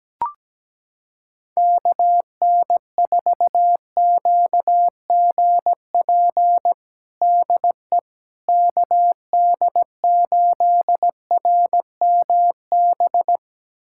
Morse code, heard as a keyed tone, about 17 wpm, 700 Hz. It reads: KN4QGP DE KD8RMB
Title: USA Callsign DE Callsign 17wpm